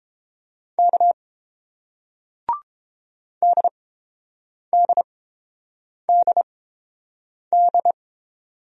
Morse code, heard as a keyed tone, about 30 wpm, spaced as for 7 wpm, 700 Hz. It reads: K DDDD